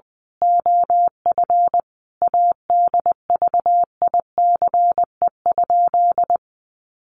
Morse code, heard as a keyed tone, about 20 wpm, 700 Hz. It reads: OF ADVICE?
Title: Sentences from Top 1100 Words 20wpm